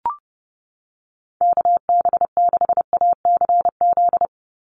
Morse code, heard as a keyed tone, about 29 wpm, 700 Hz. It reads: KB6ACZ
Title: USA Call Signs 30wpm